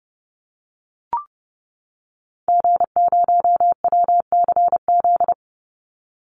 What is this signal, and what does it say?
Morse code, heard as a keyed tone, about 30 wpm, 700 Hz. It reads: G0WCZ
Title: Call Signs - 30wpm